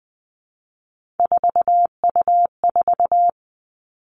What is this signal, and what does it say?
Morse code, heard as a keyed tone, about 20 wpm, 700 Hz. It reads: 4U4
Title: Three Challenging Letter-Number Combinations 20wpm